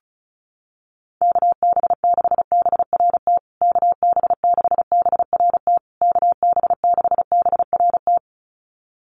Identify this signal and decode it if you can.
Morse code, heard as a keyed tone, about 34 wpm, 700 Hz. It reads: KB6BRT KB6BRT KB6BRT
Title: USA Call Signs Repeated 3x 35wpm